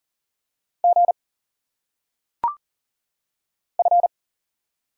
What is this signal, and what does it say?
Morse code, heard as a keyed tone, about 38 wpm, 700 Hz. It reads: G F
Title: Single Letters - Phonetic Pronunciation 40wpm